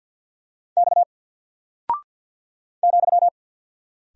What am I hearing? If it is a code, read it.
Morse code, heard as a keyed tone, about 49 wpm, 700 Hz. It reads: X ,
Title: Single Character (Letter:Number:Punctuation:Prosign) 50wpm